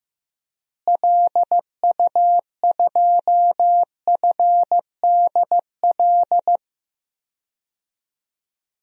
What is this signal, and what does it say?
Morse code, heard as a keyed tone, about 15 wpm, 700 Hz. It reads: LU2FDL
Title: DXCC-100 Call Signs - 6 Characters - Repeated 3x - 15wpm